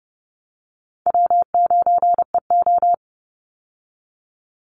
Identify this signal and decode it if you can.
Morse code, heard as a keyed tone, about 30 wpm, 700 Hz. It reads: W9EO
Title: Call Signs Repeated 2x - 30wpm